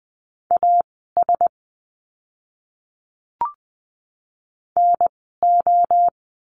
Morse code, heard as a keyed tone, about 20 wpm, spaced as for 14 wpm, 700 Hz. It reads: AS NO
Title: 054 Lesson 10f - Letter r - Words 20wpm